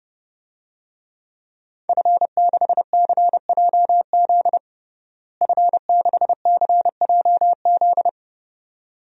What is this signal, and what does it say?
Morse code, heard as a keyed tone, about 29 wpm, 700 Hz. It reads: F6CJZ F6CJZ
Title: DXCC-100 Call Signs - 5 Characters - Repeated 2x - 30wpm